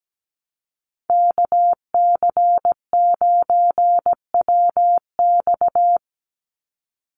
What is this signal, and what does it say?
Morse code, heard as a keyed tone, about 17 wpm, 700 Hz. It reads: KC9WX
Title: USA Call Signs 17wpm